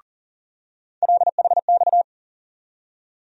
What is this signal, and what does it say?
Morse code, heard as a keyed tone, about 37 wpm, 700 Hz. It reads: LHX
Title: USA Call Sign Postfixes - 3 Letters 40wpm